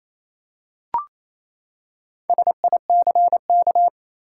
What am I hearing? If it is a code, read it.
Morse code, heard as a keyed tone, about 28 wpm, 700 Hz. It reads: SICK
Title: Four Letter Words - 28wpm